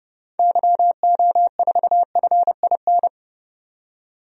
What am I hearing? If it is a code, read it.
Morse code, heard as a keyed tone, about 30 wpm, 700 Hz. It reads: YO4FIN
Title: DXCC-100 Call Signs - 6 Characters - 30wpm